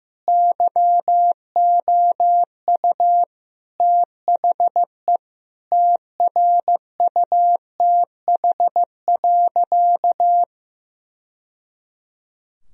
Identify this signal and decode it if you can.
Morse code, heard as a keyed tone, about 15 wpm, 700 Hz. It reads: YOU THE TRUTH.
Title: Sentences from Top 1000 Words - Review Part 2 - 15wpm